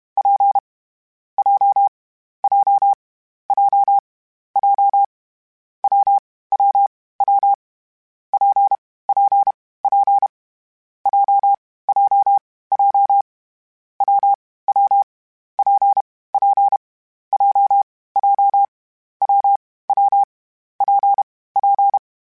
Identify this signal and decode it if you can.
Morse code, heard as a keyed tone, about 32 wpm, 800 Hz. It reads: P J J J J W W W P P P J J J W W P P J J W W P P
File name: Difficult Combos / Difficult Combos-mp3s / WPJ WPJ1 @30